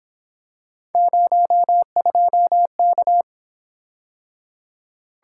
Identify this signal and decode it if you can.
Morse code, heard as a keyed tone, about 26 wpm, 700 Hz. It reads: 02K
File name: Speed-Racing - Three Letter-Number Combinations - 22wpm